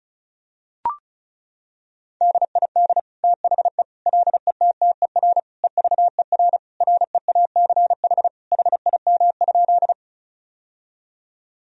Morse code, heard as a keyed tone, about 36 wpm, 700 Hz. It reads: DID THE LETTER EVER REACH HIM?
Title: Sentences from Top 1000 Words - Review Part 2 - 35wpm